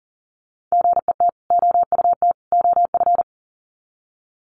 Morse code, heard as a keyed tone, about 40 wpm, 700 Hz. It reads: GET OUT OF
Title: Sets of 3 Words 40wpm